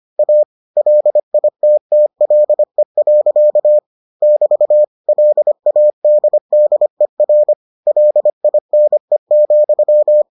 Morse code, heard as a keyed tone, about 25 wpm, 600 Hz. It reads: A LITTLE. <BT> LADDER LINE,